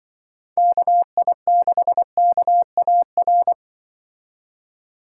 Morse code, heard as a keyed tone, about 24 wpm, 700 Hz. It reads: KI6KAR
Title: Speed Racing - USA 2x3 Call Signs 20wpm